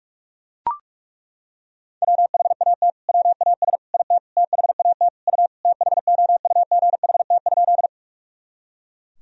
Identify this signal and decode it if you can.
Morse code, heard as a keyed tone, about 43 wpm, 700 Hz. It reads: WHAT WAS IT THAT U THOUGHT?
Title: Sentences from Top 300 Words and Abbreviations 45wpm